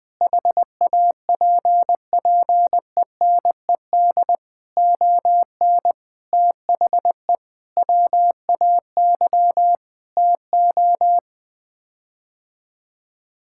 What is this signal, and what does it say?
Morse code, heard as a keyed tone, about 21 wpm, 700 Hz. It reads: HAPPENED ON THE WAY TO
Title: Sets of 8 Words - 20wpm